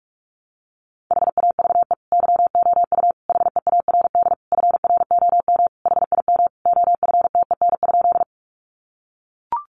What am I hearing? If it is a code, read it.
Morse code, heard as a keyed tone, about 44 wpm, 700 Hz. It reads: HAVE YOU HEARD FROM HIM OFTEN?
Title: Sentences from Top 400 Words 45wpm